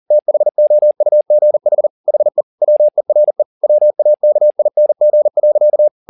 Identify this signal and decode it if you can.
Morse code, heard as a keyed tone, about 40 wpm, 600 Hz. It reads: THOUGH HE WERE WAKING.
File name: the old man and the sea 2 40wpm